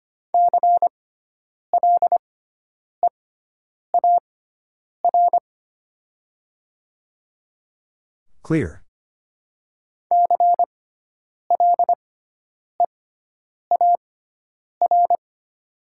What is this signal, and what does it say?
Morse code, heard as a keyed tone, about 25 wpm, spaced as for 9 wpm, 700 Hz. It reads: CLEAR CLEAR
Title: 6x Character Spacing - Top 500 Words - 25wpm